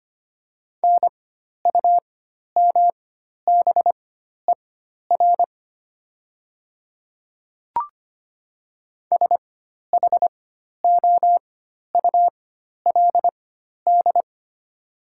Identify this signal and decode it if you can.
Morse code, heard as a keyed tone, about 25 wpm, spaced as for 12 wpm, 700 Hz. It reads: NUMBER SHOULD
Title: 4x Character Spacing - Top 300 Words - 25wpm